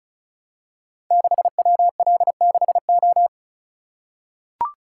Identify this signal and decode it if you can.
Morse code, heard as a keyed tone, about 36 wpm, 700 Hz. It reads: 6WL6O
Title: Code Groups - 5 Letter:Numbers - 35wpm